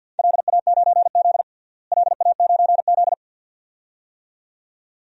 Morse code, heard as a keyed tone, about 49 wpm, 700 Hz. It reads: LA9Z LA9Z
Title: DXCC-100 Call Signs - 4 Characters - Repeated 2x - 50wpm